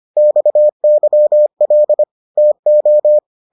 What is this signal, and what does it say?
Morse code, heard as a keyed tone, about 25 wpm, 600 Hz. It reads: XYL TO